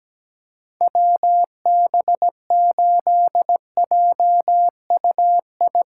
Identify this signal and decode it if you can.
Morse code, heard as a keyed tone, about 17 wpm, 700 Hz. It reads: WB8JUI
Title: K1USN SST QSO Practice - 17wpm